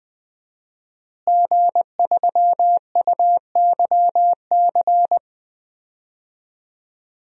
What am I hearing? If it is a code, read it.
Morse code, heard as a keyed tone, about 21 wpm, 700 Hz. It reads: G3UYC
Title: DXCC-100 Call Signs - 5 Characters - Repeated 3x - 20wpm